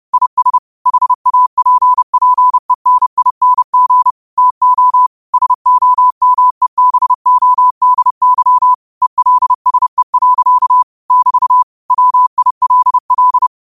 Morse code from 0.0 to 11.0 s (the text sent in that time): IS HAPPENING TO SOMEBODY ELSE.